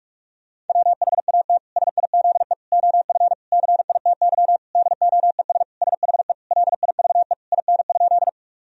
Morse code, heard as a keyed tone, about 42 wpm, 700 Hz. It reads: WHAT SIZE OF CITY DOES SHE LIVE IN?
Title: Sentences from Top 1000 Words - Review Part 2 - 45wpm